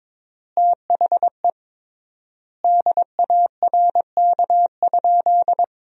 Morse code, heard as a keyed tone, about 21 wpm, 700 Hz. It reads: THE DARK?
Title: 3x Word-Spacing - Sentences from Top 500 Words - 22wpm